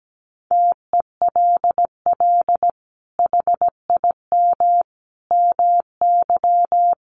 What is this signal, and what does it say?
Morse code, heard as a keyed tone, about 17 wpm, 700 Hz. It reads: TELL HIM MY
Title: Sentences from Top 1200 Words 17wpm